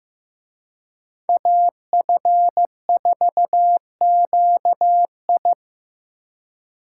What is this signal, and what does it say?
Morse code, heard as a keyed tone, about 18 wpm, 700 Hz. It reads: AF4QI
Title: Speed Racing - USA Call Signs 15wpm